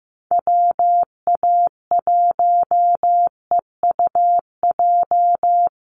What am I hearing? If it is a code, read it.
Morse code, heard as a keyed tone, about 15 wpm, 700 Hz. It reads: WA1EUJ
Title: USA Call Signs Repeated 3x 15wpm